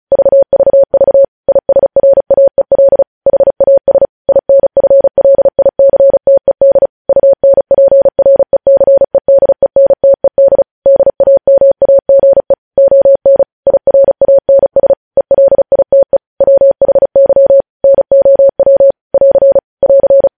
Morse code, heard as a keyed tone, about 36 wpm, 550 Hz. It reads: VVV ISRAEL HAS INFLICTED UNPRECEDENTED DAMAGE ON IRANS ELITE WHY NOW <AR> <AR>
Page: news headlines in morse code at 35 wpm